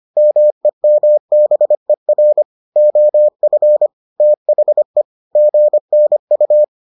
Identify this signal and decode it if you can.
Morse code, heard as a keyed tone, about 25 wpm, 600 Hz. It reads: MEMBER OF THE GNU